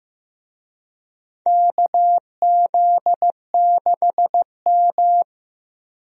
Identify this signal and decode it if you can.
Morse code, heard as a keyed tone, about 15 wpm, 700 Hz. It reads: KZ6M
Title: USA Call Signs - 4 Characters - Repeated 3x - 15wpm